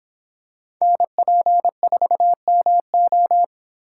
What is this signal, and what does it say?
Morse code, heard as a keyed tone, about 26 wpm, 700 Hz. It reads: NP4MO